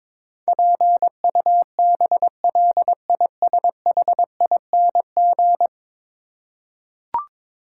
Morse code, heard as a keyed tone, about 21 wpm, 700 Hz. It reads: PUBLISHING
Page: morse-code-ninja-practice-22wpm